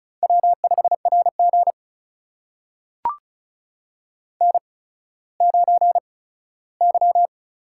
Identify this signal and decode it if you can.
Morse code, heard as a keyed tone, about 36 wpm, spaced as for 17 wpm, 700 Hz. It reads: W5RG N9Y
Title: USA Call Signs - 4 Characters - 8x Character Spacing - 35wpm